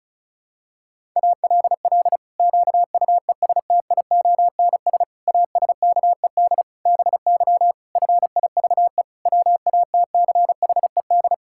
Morse code, heard as a keyed tone, about 36 wpm, 700 Hz. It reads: ALL QUESTIONS ASKED BY FIVE WATCHED
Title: Pangram Challenge v2 - 35wpm